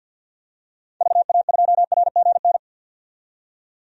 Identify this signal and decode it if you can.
Morse code, heard as a keyed tone, about 52 wpm, 700 Hz. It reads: VA2RGN
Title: DXCC-100 Call Signs - 6 Characters - 50wpm